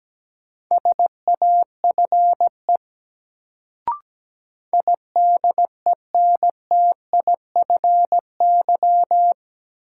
Morse code, heard as a keyed tone, about 17 wpm, 700 Hz. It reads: SAFE IDENTIFY
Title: Sentences from Top 900 Words - New Word Drill - 17wpm